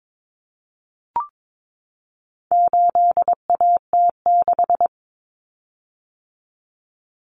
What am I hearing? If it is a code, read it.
Morse code, heard as a keyed tone, about 22 wpm, 700 Hz. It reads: 8AT6
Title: Common Cut Numbers - 4-Digit 22wpm